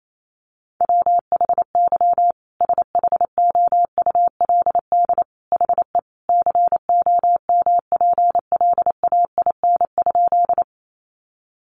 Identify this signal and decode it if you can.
Morse code, heard as a keyed tone, about 28 wpm, 700 Hz. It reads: WHY SHOULD HE COMPLAIN?
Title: Sentences from Top 1300 Words 28wpm